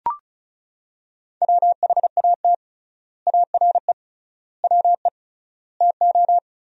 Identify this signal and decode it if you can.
Morse code, heard as a keyed tone, about 34 wpm, 700 Hz. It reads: WHAT ARE WE TO
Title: 3x Word-Spacing - Sentences from Top 100 Words - 35wpm